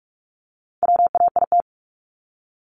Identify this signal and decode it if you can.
Morse code, heard as a keyed tone, about 44 wpm, 700 Hz. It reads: WAIT
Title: ING Suffix - Top 100 Words with Base Words 45wpm